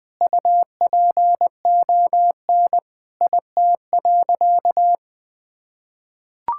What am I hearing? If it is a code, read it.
Morse code, heard as a keyed tone, about 20 wpm, 700 Hz. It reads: UPON IT.
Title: Sentences from Top 1600 Words 20wpm